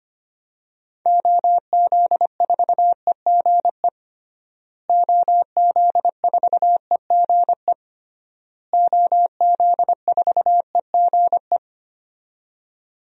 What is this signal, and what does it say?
Morse code, heard as a keyed tone, about 25 wpm, 700 Hz. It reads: OZ4EGE OZ4EGE OZ4EGE
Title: DXCC-100 Call Signs - 6 Characters - Repeated 3x - 25wpm